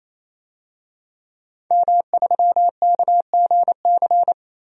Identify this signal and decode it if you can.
Morse code, heard as a keyed tone, about 27 wpm, 700 Hz. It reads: M3KGC
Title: DXCC-100 Call Signs - 5 Characters - 28wpm